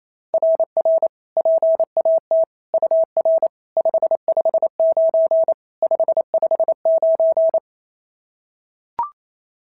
Morse code, heard as a keyed tone, about 28 wpm, 660 Hz. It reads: RR PAT UR 559 559
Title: SOTA QSO Practice 28wpm